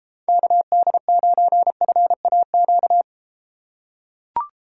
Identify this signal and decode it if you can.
Morse code, heard as a keyed tone, about 32 wpm, 700 Hz. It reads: KD9FAQ